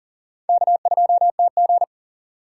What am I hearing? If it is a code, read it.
Morse code, heard as a keyed tone, about 40 wpm, 700 Hz. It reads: K2TG
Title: USA Call Signs - 4 Characters - 3x Character Spacing - 40wpm